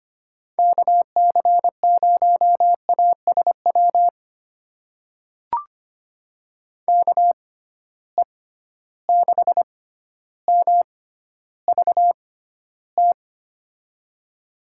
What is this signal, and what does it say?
Morse code, heard as a keyed tone, about 25 wpm, spaced as for 13 wpm, 700 Hz. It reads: KC0ASW KE6MVT
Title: USA Call Signs - 6 Characters - 6x Character Spacing - 25wpm